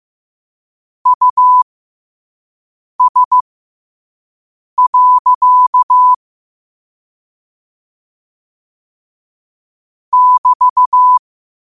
Morse code, heard as a keyed tone, about 15 wpm, spaced as for 5 wpm, 1000 Hz. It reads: US. <BT>
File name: qotdfiles / 5wpm